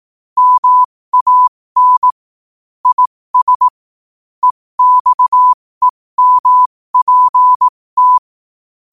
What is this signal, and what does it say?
Morse code, heard as a keyed tone, about 18 wpm, spaced as for 15 wpm, 1000 Hz. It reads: MAN IS EXEMPT
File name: qotdfiles / 15wpm